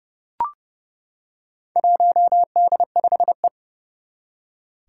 Morse code, heard as a keyed tone, about 30 wpm, 700 Hz. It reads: 1D5E
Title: Four Challenging Letter-Number Combinations 30wpm